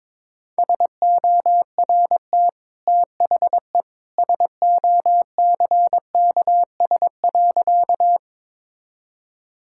Morse code, heard as a keyed tone, about 22 wpm, 700 Hz. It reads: SORT THE SOCKS.
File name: practice / 22wpm / Easy Sentences with Hints - 22wpm